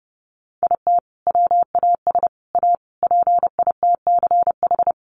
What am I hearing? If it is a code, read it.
Morse code, heard as a keyed tone, about 28 wpm, 700 Hz. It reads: IT WAS A PITCH
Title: Sentences from Top 1300 Words 30wpm